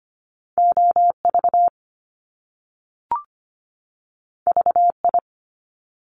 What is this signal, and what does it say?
Morse code, heard as a keyed tone, about 25 wpm, 700 Hz. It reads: OV VI
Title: Top Two Letter Combinations in Words 25wpm